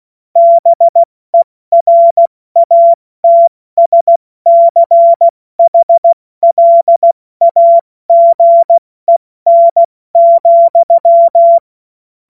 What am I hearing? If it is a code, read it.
Morse code, heard as a keyed tone, about 16 wpm, spaced as for 14 wpm, 680 Hz. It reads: BERATSCHLAGEN,